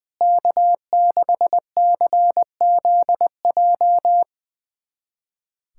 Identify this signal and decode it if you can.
Morse code, heard as a keyed tone, about 20 wpm, 700 Hz. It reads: K6CZJ
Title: USA Callsign DE Callsign - 5 Characters - 20wpm